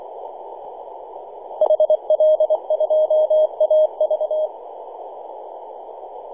Morse code, heard as a keyed tone, about 23 wpm, 620 Hz. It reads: HL2AV
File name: e51dwc-21-010mhz-cw